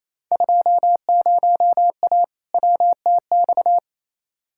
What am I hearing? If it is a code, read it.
Morse code, heard as a keyed tone, about 28 wpm, 700 Hz. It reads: 20A WTX
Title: Field Day Exchanges Encore - 28wpm